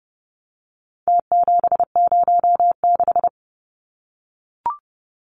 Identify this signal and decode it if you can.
Morse code, heard as a keyed tone, about 30 wpm, 700 Hz. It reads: T706
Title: Numbers - Contest Style - 4-Digit 30wpm